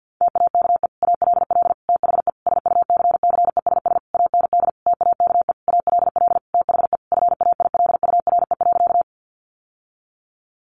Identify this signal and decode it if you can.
Morse code, heard as a keyed tone, about 50 wpm, 700 Hz. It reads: TAKE ALL THE SUCCESS AND TAKE ALL THE FAILURE.
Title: Sentences from Top 900 Words 50wpm